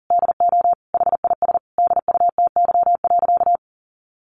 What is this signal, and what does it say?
Morse code, heard as a keyed tone, about 40 wpm, 700 Hz. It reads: DO HIS DUTY.
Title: Sentences from Top 1500 Words 40wpm